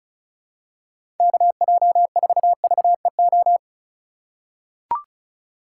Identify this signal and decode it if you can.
Morse code, heard as a keyed tone, about 36 wpm, 700 Hz. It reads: KJ4VEO
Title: USA Call Signs - 6 Characters - 3x Character Spacing - 35wpm